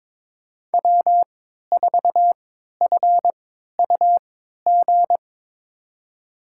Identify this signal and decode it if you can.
Morse code, heard as a keyed tone, about 22 wpm, spaced as for 13 wpm, 700 Hz. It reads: W4FUG
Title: USA Call Signs - 5 Characters - 3x Character Spacing - 22wpm